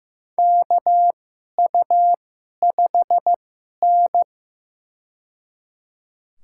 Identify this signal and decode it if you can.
Morse code, heard as a keyed tone, about 15 wpm, spaced as for 11 wpm, 700 Hz. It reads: KU5N
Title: USA Call Signs - 4 Characters - 2x Character Spacing - 15wpm